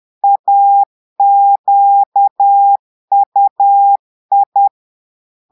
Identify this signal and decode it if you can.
Morse code, heard as a keyed tone, about 10 wpm, 800 Hz. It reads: AQUI
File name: Portugais - texte en clair 1 - vitesse - 10 mots minute